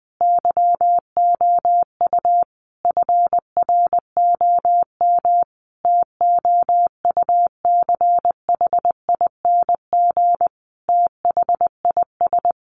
Morse code, heard as a keyed tone, about 20 wpm, 700 Hz. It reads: YOU FROM TOUCHING THIS
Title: A Sound Of Thunder 20wpm